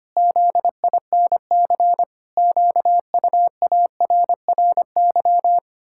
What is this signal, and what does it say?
Morse code, heard as a keyed tone, about 25 wpm, 700 Hz. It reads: ZINC QUARRY
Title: Pangram Challenge v2 - 25wpm